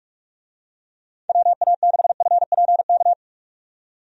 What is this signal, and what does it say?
Morse code, heard as a keyed tone, about 46 wpm, 700 Hz. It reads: WA6FPK
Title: USA Call Signs - 6 Characters - 45wpm